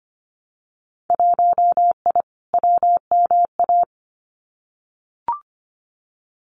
Morse code, heard as a keyed tone, about 25 wpm, 690 Hz. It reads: 1I WMA
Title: Winter Field Day Exchanges Encore 25wpm